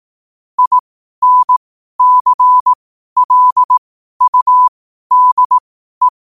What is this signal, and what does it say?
Morse code, heard as a keyed tone, about 18 wpm, spaced as for 12 wpm, 1000 Hz. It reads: INCLUDE
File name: qotdfiles / 12wpm